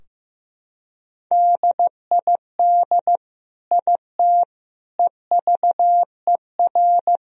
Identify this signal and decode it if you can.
Morse code, heard as a keyed tone, about 15 wpm, 700 Hz. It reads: DID IT EVER
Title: Sentences from Top 1000 Words - Review Part 2 - 15wpm